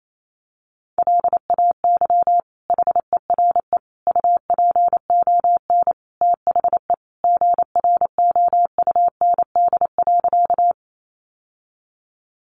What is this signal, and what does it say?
Morse code, heard as a keyed tone, about 28 wpm, 700 Hz. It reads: LAY HERE UPON THE GROUND.
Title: Sentences from Top 1100 Words 28wpm